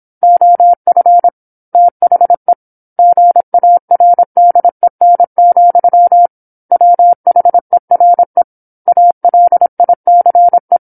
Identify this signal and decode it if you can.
Morse code, heard as a keyed tone, about 27 wpm, 700 Hz. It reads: OF THE GARDEN, WHERE ALICE
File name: 08-Chapter-08 26 WPM